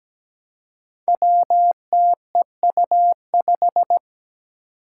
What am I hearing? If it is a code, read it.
Morse code, heard as a keyed tone, about 17 wpm, 700 Hz. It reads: WTEU5
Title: Five Challenging Letter-Number Combinations 17wpm